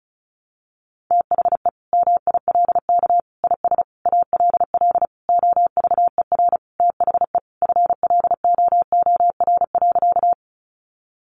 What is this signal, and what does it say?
Morse code, heard as a keyed tone, about 34 wpm, 700 Hz. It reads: THE MILK IS ALL OVER THE FLOOR.
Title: Additional Easy Phrases 35wpm